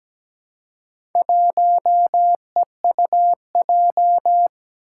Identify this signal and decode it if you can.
Morse code, heard as a keyed tone, about 17 wpm, 700 Hz. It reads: 1EUJ
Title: Four Challenging Letter-Number Combinations 17wpm